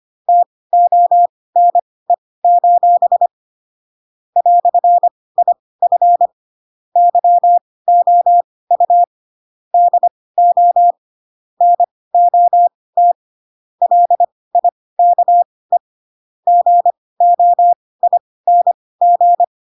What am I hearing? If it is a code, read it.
Morse code, heard as a keyed tone, about 25 wpm, spaced as for 18 wpm, 700 Hz. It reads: TONE: "IF YOU DO NOT LIKE GOING
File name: Pinocchio / Pinocchio ch4 12 25 ik1hge